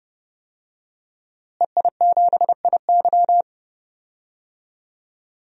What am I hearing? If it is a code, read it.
Morse code, heard as a keyed tone, about 30 wpm, 700 Hz. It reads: EI7IY
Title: DXCC-100 Call Signs - 5 Characters - Repeated 3x - 30wpm